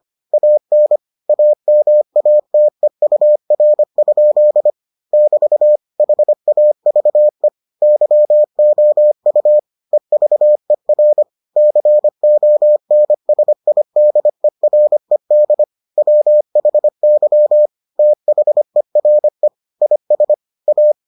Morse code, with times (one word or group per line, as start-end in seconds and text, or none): 0.3-1.0 AN
1.3-4.7 AMATEUR?
5.1-5.8 <BT>
6.0-7.5 HAVE
7.8-9.6 YOU
9.9-11.2 EVER
11.6-15.6 CONSIDERED
16.0-17.7 WHY
18.0-19.5 THERE
19.8-20.3 IS
20.7-20.9 A